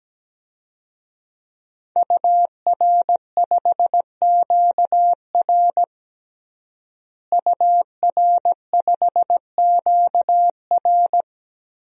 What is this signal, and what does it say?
Morse code, heard as a keyed tone, about 17 wpm, 700 Hz. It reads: UR5QR UR5QR
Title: DXCC-100 Call Signs - 5 Characters - Repeated 3x - 17wpm